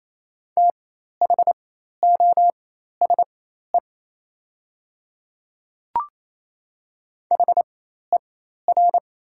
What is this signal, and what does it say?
Morse code, heard as a keyed tone, about 27 wpm, spaced as for 13 wpm, 690 Hz. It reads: THOSE HER
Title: 4x Character Spacing - Top 500 Words - 28wpm